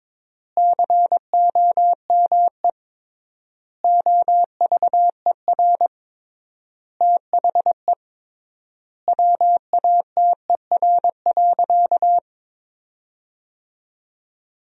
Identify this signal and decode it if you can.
Morse code, heard as a keyed tone, about 21 wpm, 700 Hz. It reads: COME OVER THE WATER.
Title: 3x Word-Spacing - Sentences from Top 300 Words - 22wpm